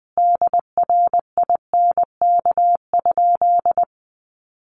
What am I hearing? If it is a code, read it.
Morse code, heard as a keyed tone, about 20 wpm, 700 Hz. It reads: DRINK?
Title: Sentences from Top 1500 Words 20wpm